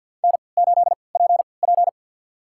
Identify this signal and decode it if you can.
Morse code, heard as a keyed tone, about 48 wpm, spaced as for 28 wpm, 700 Hz. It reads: N8PP
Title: USA Call Signs - 4 Characters - 3x Character Spacing - 50wpm